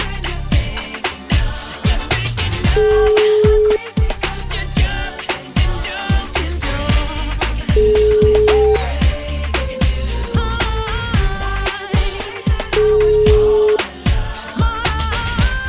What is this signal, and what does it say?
Ringback tone (Europe, 425 Hz), with noise.